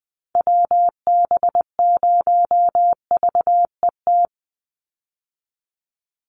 Morse code, heard as a keyed tone, about 20 wpm, 700 Hz. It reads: WB0VET
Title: USA Call Signs 20wpm